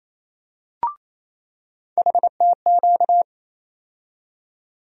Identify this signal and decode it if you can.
Morse code, heard as a keyed tone, about 27 wpm, 700 Hz. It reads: HTQ
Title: USA Call Sign Postfixes - 3 Letters 28wpm